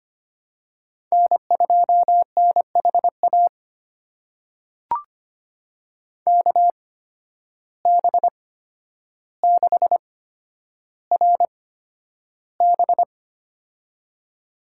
Morse code, heard as a keyed tone, about 25 wpm, spaced as for 11 wpm, 700 Hz. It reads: N2NHA KB6RB
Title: USA Call Signs - 5 Characters - 8x Character Spacing - 25wpm